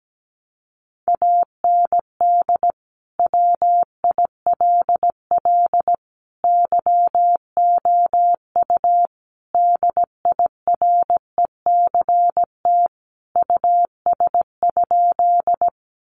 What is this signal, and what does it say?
Morse code, heard as a keyed tone, about 17 wpm, 700 Hz. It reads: AND WILL YOU DIRECT US?
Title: Sentences from Top 400 Words 17wpm